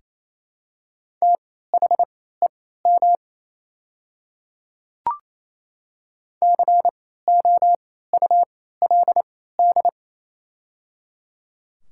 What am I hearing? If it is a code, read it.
Morse code, heard as a keyed tone, about 28 wpm, spaced as for 16 wpm, 700 Hz. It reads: THEM COULD
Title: 3x Character Spacing - Top 200 Words - 28wpm